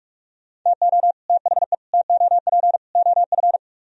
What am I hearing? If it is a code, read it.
Morse code, heard as a keyed tone, about 58 wpm, spaced as for 51 wpm, 700 Hz. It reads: TO THE TOP OF
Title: Speed-Racing - Sets of 4 Words 45wpm